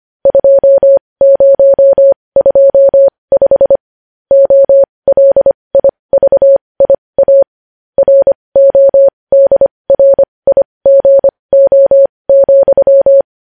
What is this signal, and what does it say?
Morse code, heard as a keyed tone, about 25 wpm, spaced as for 20 wpm, 550 Hz. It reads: 2025 OLIVIA RODRIGO,
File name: podcast / News headlines-1x-25x20